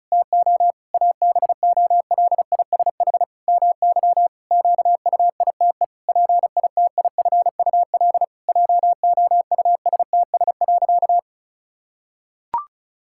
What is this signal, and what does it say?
Morse code, heard as a keyed tone, about 34 wpm, 700 Hz. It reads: TO ABOLISH MY QUITE PITIFUL JOUSTS.
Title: Pangram Challenge v2 - 35wpm